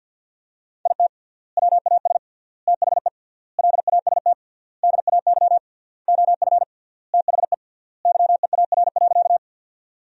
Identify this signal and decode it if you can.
Morse code, heard as a keyed tone, about 52 wpm, 700 Hz. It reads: IT WAS THE LAST DAY OF THE YEAR.
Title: 3x Word-Spacing - Sentences from Top 300 Words - 50wpm